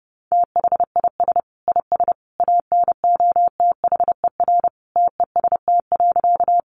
Morse code, heard as a keyed tone, about 30 wpm, 700 Hz. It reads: THIS IS ANOTHER TEST.
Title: Sentences from Top 600 Words 30wpm